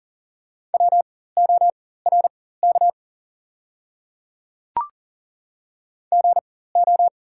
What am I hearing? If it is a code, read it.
Morse code, heard as a keyed tone, about 39 wpm, spaced as for 18 wpm, 700 Hz. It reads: WORK GO
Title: 4x Character Spacing - Top 200 Words - 40wpm